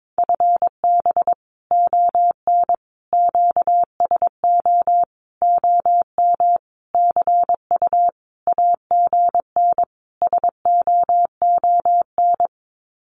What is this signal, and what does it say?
Morse code, heard as a keyed tone, about 22 wpm, 700 Hz. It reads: FB ON QSO OM CU AGN SOON
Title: Common QSO Phrases 22wpm